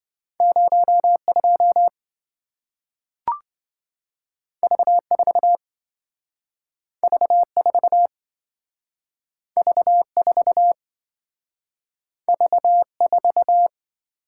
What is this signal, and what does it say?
Morse code, heard as a keyed tone, about 27 wpm, 700 Hz. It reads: 02 V4 V4 V4 V4